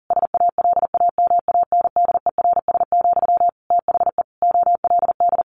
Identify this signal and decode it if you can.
Morse code, heard as a keyed tone, about 40 wpm, 700 Hz. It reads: SALAMANDERS, THE OLD
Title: A Sound Of Thunder 40wpm